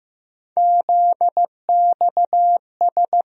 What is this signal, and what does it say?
Morse code, heard as a keyed tone, about 15 wpm, 700 Hz. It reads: ZXS
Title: USA Call Sign Postfixes - 3 Letters 15wpm